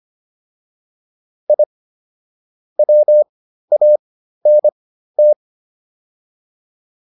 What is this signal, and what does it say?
Morse code, heard as a keyed tone, about 25 wpm, spaced as for 13 wpm, 600 Hz. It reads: I WANT